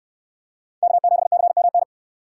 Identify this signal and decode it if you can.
Morse code, heard as a keyed tone, about 64 wpm, 700 Hz. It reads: K7CMN